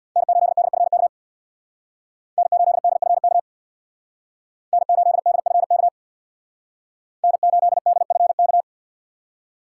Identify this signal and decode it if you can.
Morse code, heard as a keyed tone, about 66 wpm, 700 Hz. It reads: N8DFK N8DFK N8DFK N8DFK